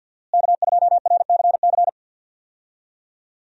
Morse code, heard as a keyed tone, about 52 wpm, 700 Hz. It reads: K2RCC
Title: USA Call Signs - 5 Characters - Repeated 2x - 50wpm